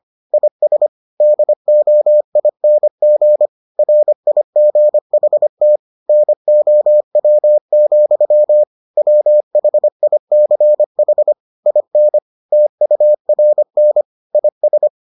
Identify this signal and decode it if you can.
Morse code, heard as a keyed tone, about 25 wpm, 600 Hz. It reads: IS DOING RIGHT NOW, WHICH IN TURN IS